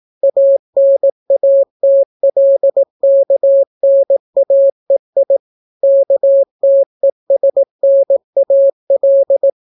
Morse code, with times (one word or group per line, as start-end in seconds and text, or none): 0.2-5.4 ANATLKNAEI
5.8-9.5 KTESNAL